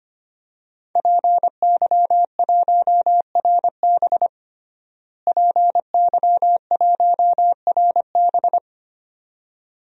Morse code, heard as a keyed tone, about 25 wpm, 700 Hz. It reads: PY1RB PY1RB
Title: DXCC-100 Call Signs - 5 Characters - Repeated 2x - 25wpm